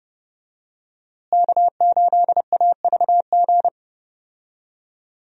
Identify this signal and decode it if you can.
Morse code, heard as a keyed tone, about 30 wpm, 700 Hz. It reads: K8AVG
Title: USA Call Signs - 5 Characters - Repeated 3x - 30wpm